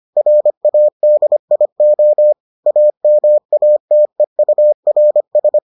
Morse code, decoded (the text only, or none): RADIO AMATEURS